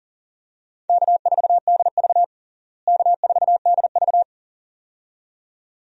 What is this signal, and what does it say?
Morse code, heard as a keyed tone, about 40 wpm, 700 Hz. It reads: K4DV K4DV
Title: USA Call Signs - 4 Characters - Repeated 2x - 40wpm